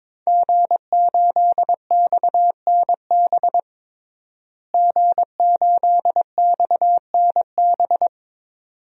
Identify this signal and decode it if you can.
Morse code, heard as a keyed tone, about 22 wpm, 700 Hz. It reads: G8XNB G8XNB
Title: DXCC-100 Call Signs - 5 Characters - Repeated 2x - 22wpm